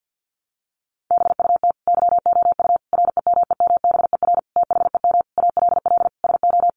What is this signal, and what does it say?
Morse code, heard as a keyed tone, about 50 wpm, 700 Hz. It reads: BUT YOU REMEMBER THEM ALL SO
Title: Sentences from Top 1300 Words 50wpm